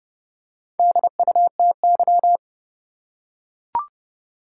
Morse code, heard as a keyed tone, about 30 wpm, 700 Hz. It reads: DUTY
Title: Four Letter Words - 30wpm